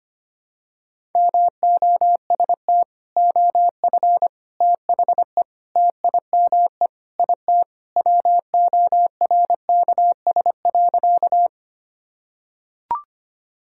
Morse code, heard as a keyed tone, about 25 wpm, 700 Hz. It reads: MOST OF THE TIME IT WORKS.
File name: practice / 25wpm / Sentences with 3-Word Phrases - 25wpm